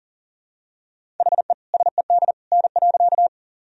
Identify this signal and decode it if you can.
Morse code, heard as a keyed tone, about 40 wpm, 700 Hz. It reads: HE SED N.
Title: Sentences from Top 100 Words and Abbreviations 40wpm